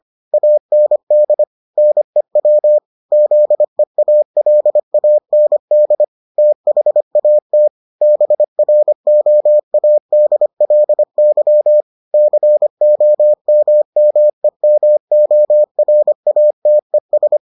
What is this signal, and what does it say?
Morse code, heard as a keyed tone, about 25 wpm, 600 Hz. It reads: AND NEW ZEALAND THAT BROADLY COMMEMORATES